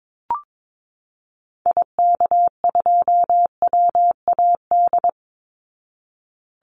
Morse code, heard as a keyed tone, about 22 wpm, 700 Hz. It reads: IK2WAD
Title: Call Signs Repeated 2x - 22wpm